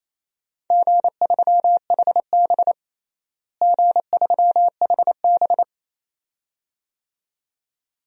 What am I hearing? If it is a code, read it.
Morse code, heard as a keyed tone, about 27 wpm, 700 Hz. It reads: G3HB G3HB
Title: DXCC-100 Call Signs - 4 Characters - Repeated 2x - 28wpm